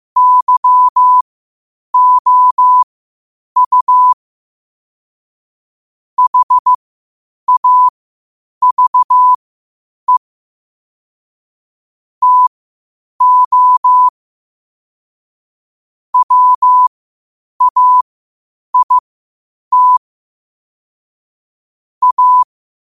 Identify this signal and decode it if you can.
Morse code, heard as a keyed tone, about 15 wpm, spaced as for 8 wpm, 1000 Hz. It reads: YOU HAVE TO WAIT A